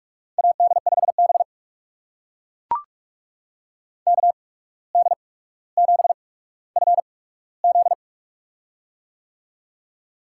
Morse code, heard as a keyed tone, about 44 wpm, spaced as for 18 wpm, 700 Hz. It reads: AD5B KD7FZ
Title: USA Call Signs - 8x Character Spacing - 45wpm